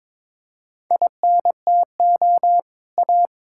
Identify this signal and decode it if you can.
Morse code, heard as a keyed tone, about 22 wpm, 700 Hz. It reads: INTO A
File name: practice / 22wpm / Sentences with 2-Word Phrases - 22wpm